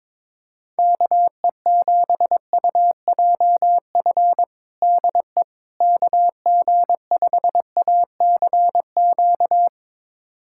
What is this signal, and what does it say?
Morse code, heard as a keyed tone, about 22 wpm, 700 Hz. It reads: KE7UJF DE KG5ACQ
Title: USA Callsign DE Callsign - 6 Characters - 22wpm